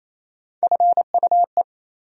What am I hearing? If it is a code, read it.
Morse code, heard as a keyed tone, about 27 wpm, 700 Hz. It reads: FUE
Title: USA Call Sign Postfixes - 3 Letters 28wpm